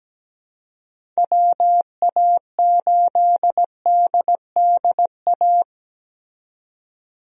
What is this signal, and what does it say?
Morse code, heard as a keyed tone, about 17 wpm, 700 Hz. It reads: WA8DDA
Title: USA Call Signs - 6 Characters - 17wpm